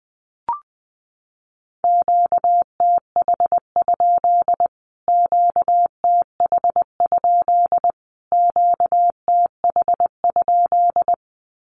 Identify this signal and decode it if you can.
Morse code, heard as a keyed tone, about 20 wpm, 700 Hz. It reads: QTH? QTH? QTH?
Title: Q-Codes Repeated 3x 20wpm